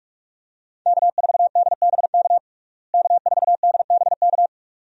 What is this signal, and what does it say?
Morse code, heard as a keyed tone, about 46 wpm, 700 Hz. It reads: K4DBK K4DBK
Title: USA Call Signs - 5 Characters - Repeated 2x - 45wpm